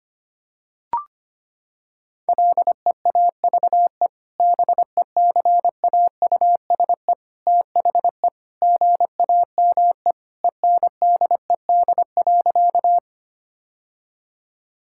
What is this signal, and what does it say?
Morse code, heard as a keyed tone, about 25 wpm, 700 Hz. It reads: LEAVE BECAUSE THE GAME ENDED.
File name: practice / 25wpm / Sentences with 2-Word Phrases - 25wpm